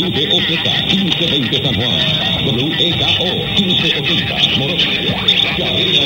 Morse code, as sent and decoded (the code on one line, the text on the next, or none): none